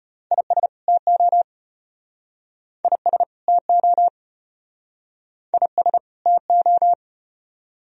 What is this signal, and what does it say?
Morse code, heard as a keyed tone, about 34 wpm, 700 Hz. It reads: IS TO IS TO IS TO